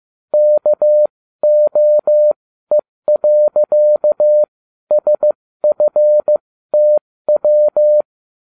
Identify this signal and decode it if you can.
Morse code, heard as a keyed tone, about 15 wpm, spaced as for 13 wpm, 600 Hz. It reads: KOE.SFTW